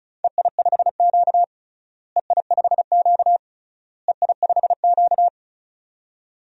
Morse code, heard as a keyed tone, about 36 wpm, 700 Hz. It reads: EI5Q EI5Q EI5Q
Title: DXCC-100 Call Signs - 4 Characters - Repeated 3x - 35wpm